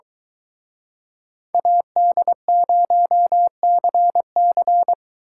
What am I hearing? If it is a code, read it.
Morse code, heard as a keyed tone, about 23 wpm, 700 Hz. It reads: AD0CC